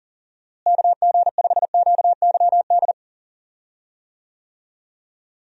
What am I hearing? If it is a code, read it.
Morse code, heard as a keyed tone, about 40 wpm, 700 Hz. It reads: KG5QYD
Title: USA Call Signs - 6 Characters - Repeated 3x - 40wpm